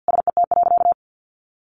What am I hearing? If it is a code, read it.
Morse code, heard as a keyed tone, about 50 wpm, 700 Hz. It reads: SET.
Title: Additional Easy Phrases 50wpm